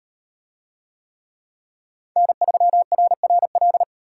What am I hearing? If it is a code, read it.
Morse code, heard as a keyed tone, about 38 wpm, 700 Hz. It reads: N3RRL